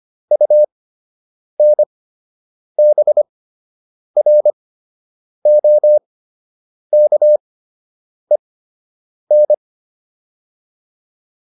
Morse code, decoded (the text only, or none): UNBROKEN